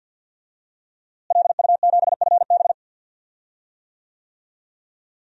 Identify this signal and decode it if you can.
Morse code, heard as a keyed tone, about 52 wpm, 700 Hz. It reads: LU7FB
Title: DXCC-100 Call Signs - 5 Characters - Repeated 3x - 50wpm